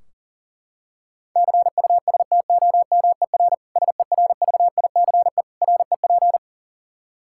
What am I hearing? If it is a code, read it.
Morse code, heard as a keyed tone, about 40 wpm, 700 Hz. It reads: CUSTOMER SERVICE REP
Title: Practice QSO - Exchange 3 - More Info - 40wpm